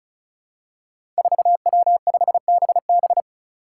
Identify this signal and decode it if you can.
Morse code, heard as a keyed tone, about 34 wpm, 700 Hz. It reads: 4W5BB
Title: Five Challenging Letter-Number Combinations 35wpm